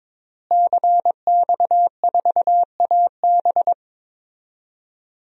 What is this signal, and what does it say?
Morse code, heard as a keyed tone, about 22 wpm, 700 Hz. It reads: CX4AB
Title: DXCC-100 Call Signs - 5 Characters - Repeated 3x - 22wpm